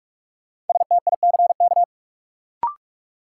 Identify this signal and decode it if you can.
Morse code, heard as a keyed tone, about 42 wpm, 700 Hz. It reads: STICK
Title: Sentences from Top 1200 Words - New Word Drill - 45wpm